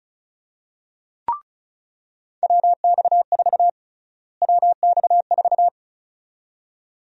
Morse code, heard as a keyed tone, about 34 wpm, 700 Hz. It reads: WX4 WX4
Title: USA Call Sign Prefixes - Repeated 2x - 35wpm